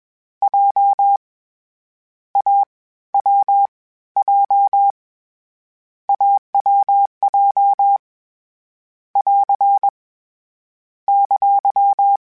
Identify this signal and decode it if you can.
Morse code, heard as a keyed tone, about 21 wpm, 800 Hz. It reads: J A W J AWJ <AR> !
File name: CW Code Groups-mp3s / Sets mp3s / AUV sp @20